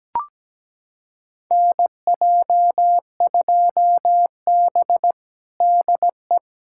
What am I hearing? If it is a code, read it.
Morse code, heard as a keyed tone, about 17 wpm, 700 Hz. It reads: NJ2B DE
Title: USA Callsign DE Callsign - 4 Characters - 17wpm